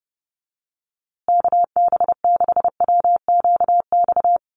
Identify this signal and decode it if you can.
Morse code, heard as a keyed tone, about 30 wpm, 700 Hz. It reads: KB6WQX
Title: USA Call Signs Repeated 3x 30wpm